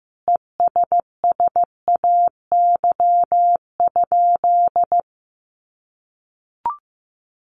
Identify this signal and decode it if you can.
Morse code, heard as a keyed tone, about 14 wpm, 700 Hz. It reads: ESSAY?
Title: Sentences from Top 1500 Words 15wpm